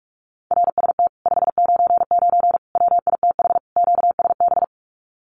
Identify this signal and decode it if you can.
Morse code, heard as a keyed tone, about 46 wpm, 700 Hz. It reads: RST 599 WITH QSB
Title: Common QSO Phrases Encore 45wpm